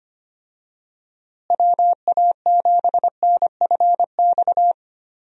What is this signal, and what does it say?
Morse code, heard as a keyed tone, about 25 wpm, 700 Hz. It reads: WA7NFX
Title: USA Call Signs - 6 Characters - Repeated 2x - 25wpm